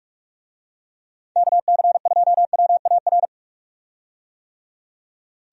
Morse code, heard as a keyed tone, about 43 wpm, 700 Hz. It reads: KC2WAR
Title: USA Call Signs - 6 Characters - 45wpm